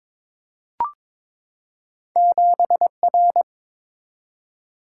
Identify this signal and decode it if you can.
Morse code, heard as a keyed tone, about 22 wpm, 690 Hz. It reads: 7R
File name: practice / 22wpm / Two Letter-Number Combinations - 22wpm